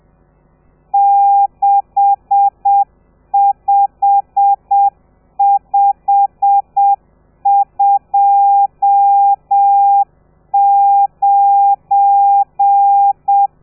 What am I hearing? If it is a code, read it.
Morse code, heard as a keyed tone, about 7 wpm, 800 Hz. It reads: 65529
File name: telegrafia / i2tfj / Lez12 30